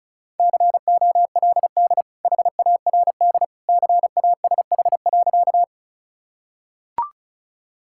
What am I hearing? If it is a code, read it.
Morse code, heard as a keyed tone, about 34 wpm, 700 Hz. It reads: COLD HARD CASH.
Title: Sentences from Top 1000 Words - Review Part 5 - 35wpm